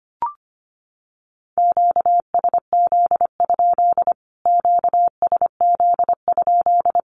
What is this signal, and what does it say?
Morse code, heard as a keyed tone, about 25 wpm, 700 Hz. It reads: QSZ? QSZ?
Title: Q-Codes Repeated 3x 25wpm